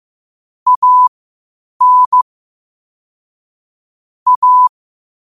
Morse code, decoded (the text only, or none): AN A